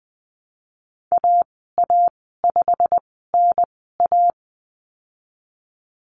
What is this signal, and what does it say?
Morse code, heard as a keyed tone, about 19 wpm, spaced as for 14 wpm, 700 Hz. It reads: AA5NA
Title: 173 Lesson 28g - Number 9 - Call Signs 20wpm